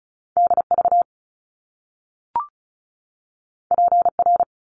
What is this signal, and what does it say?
Morse code, heard as a keyed tone, about 36 wpm, 700 Hz. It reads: DV PR
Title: Top Two Letter Combinations in Words 35wpm